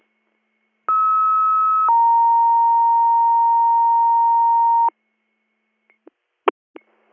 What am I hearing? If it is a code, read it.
Two-tone page, heard as a one-second tone, then a three-second tone, about 1290 then 930 Hz. It is heard from 0.9 to 4.9 s.